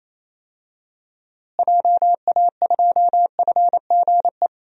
Morse code, heard as a keyed tone, about 28 wpm, 700 Hz. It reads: JA2FGE
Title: DXCC-100 Call Signs - 6 Characters - Repeated 2x - 28wpm